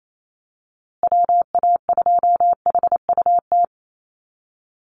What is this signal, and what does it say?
Morse code, heard as a keyed tone, about 28 wpm, 700 Hz. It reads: WA2HUT
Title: USA Call Signs Repeated 2x 28wpm